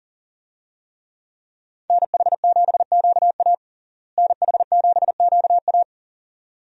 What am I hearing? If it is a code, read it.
Morse code, heard as a keyed tone, about 40 wpm, 700 Hz. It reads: NH7QA NH7QA
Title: USA Call Signs - 5 Characters - Repeated 2x - 40wpm